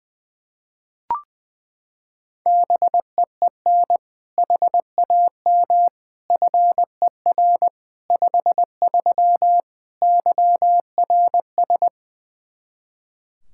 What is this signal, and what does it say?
Morse code, heard as a keyed tone, about 20 wpm, 700 Hz. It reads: BEEN HAM FER 53 YRS
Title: Practice QSO - Exchange 3 - More Info - 20wpm